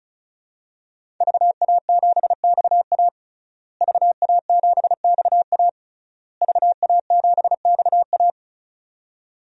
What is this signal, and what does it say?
Morse code, heard as a keyed tone, about 34 wpm, 700 Hz. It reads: VA7XA VA7XA VA7XA
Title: DXCC-100 Call Signs - 5 Characters - Repeated 3x - 35wpm